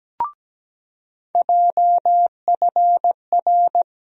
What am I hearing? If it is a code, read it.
Morse code, heard as a keyed tone, about 17 wpm, 700 Hz. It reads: JFR
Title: USA Call Sign Postfixes - 3 Letters 17wpm